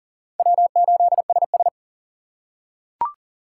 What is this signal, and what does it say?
Morse code, heard as a keyed tone, about 40 wpm, 700 Hz. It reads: W8SS
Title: USA Call Signs - 4 Characters - 3x Character Spacing - 40wpm